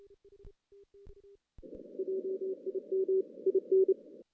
Morse code, heard as a keyed tone, about 30 wpm, 380 Hz. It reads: JIM IN
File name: 2016 JAN CW NAQP / 2016 Jan naqp k5ot - 0-10